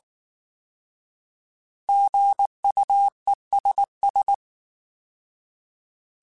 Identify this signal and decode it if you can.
Morse code, heard as a keyed tone, about 19 wpm, 800 Hz. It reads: GUESS